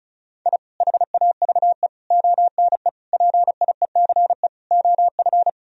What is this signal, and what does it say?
Morse code, heard as a keyed tone, about 36 wpm, 700 Hz. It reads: I HAVE ONE PIECE OF
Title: Sentences from Top 1000 Words - Review Part 2 - 35wpm